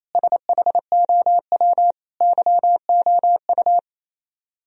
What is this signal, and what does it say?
Morse code, heard as a keyed tone, about 34 wpm, 700 Hz. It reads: SHOW YOU
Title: Speed-Racing - Sets of 5 Words 28wpm